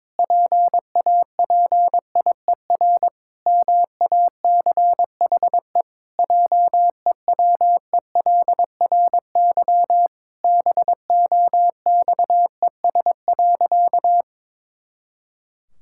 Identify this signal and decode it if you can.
Morse code, heard as a keyed tone, about 22 wpm, 700 Hz. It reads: PAPIER MACHE JEWELRY BOXES.
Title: Pangram Challenge v2 - 22wpm